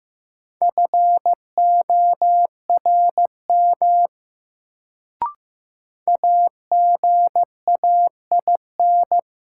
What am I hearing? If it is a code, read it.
Morse code, heard as a keyed tone, about 15 wpm, 700 Hz. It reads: FORM AGAIN
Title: Sets of 4 Words - New Word Drill - 15wpm